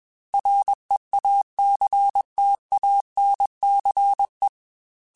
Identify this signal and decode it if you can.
Morse code, heard as a keyed tone, about 21 wpm, 800 Hz. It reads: REACTANCE
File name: Technical Subjects / Technical Subjects-mp3s / XC @20